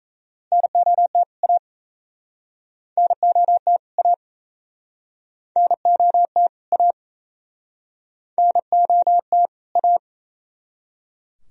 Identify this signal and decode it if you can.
Morse code, heard as a keyed tone, about 38 wpm, 700 Hz. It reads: NOT A NOT A NOT A NOT A